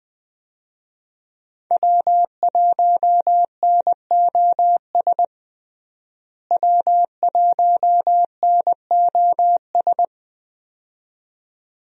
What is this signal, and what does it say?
Morse code, heard as a keyed tone, about 20 wpm, 700 Hz. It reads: W1NOS W1NOS
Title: USA Call Signs - 5 Characters - Repeated 2x - 20wpm